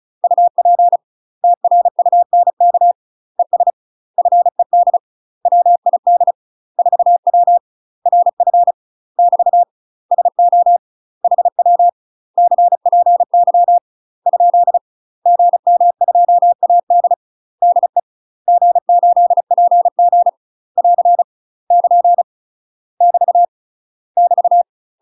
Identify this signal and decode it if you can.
Morse code, heard as a keyed tone, about 34 wpm, 700 Hz. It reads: UP TRUNK ES FED WID 4W RF <BT> SO HW CPY ? GM2AD DE G8PG <AR> <KN> <BT> <BT>
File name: UK MASTER QSOs TRAINING / 35WPM / UK MASTER QSOs 0000